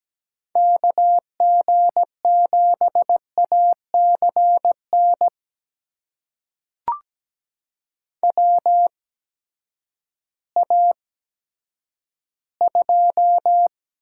Morse code, heard as a keyed tone, about 17 wpm, spaced as for 10 wpm, 700 Hz. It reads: KG7ACN WA2
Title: USA Call Signs - 6 Characters - 8x Character Spacing - 17wpm